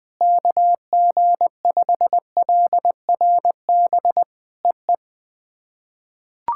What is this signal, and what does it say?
Morse code, heard as a keyed tone, about 20 wpm, 700 Hz. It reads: KG5LRB EE
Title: POTA QSO Practice - 20wpm